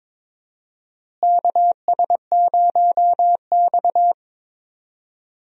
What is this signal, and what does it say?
Morse code, heard as a keyed tone, about 22 wpm, 700 Hz. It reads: KS0X
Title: USA Call Signs - 4 Characters - Repeated 2x - 22wpm